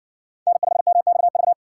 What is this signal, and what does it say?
Morse code, heard as a keyed tone, about 56 wpm, 700 Hz. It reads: N5MXV